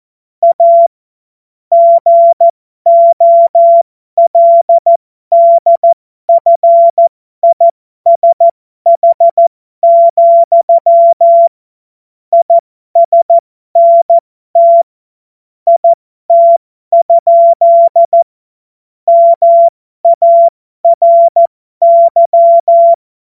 Morse code, heard as a keyed tone, about 14 wpm, spaced as for 12 wpm, 680 Hz. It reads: A GOLDFISH, ISNT IT? MARY